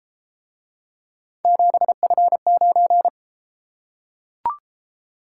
Morse code, heard as a keyed tone, about 33 wpm, 700 Hz. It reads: 7F9